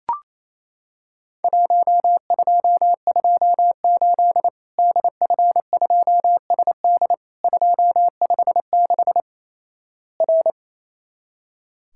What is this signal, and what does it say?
Morse code, heard as a keyed tone, about 27 wpm, 700 Hz. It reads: 1228 DF2SD 256 R
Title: Worked All Europe DX Contest - QTC Exchange - 28wpm